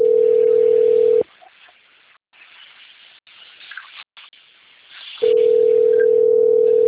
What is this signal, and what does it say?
Ringback tone (North America), 440 + 480 Hz.
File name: out- - - - - .12561